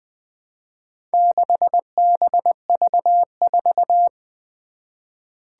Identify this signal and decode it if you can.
Morse code, heard as a keyed tone, about 20 wpm, 700 Hz. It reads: 6BV4
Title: Four Challenging Letter-Number Combinations 20wpm